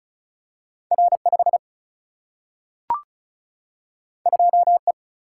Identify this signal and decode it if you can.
Morse code, heard as a keyed tone, about 36 wpm, 700 Hz. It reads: R 5 2 E
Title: Single-Letter Single-Number 35wpm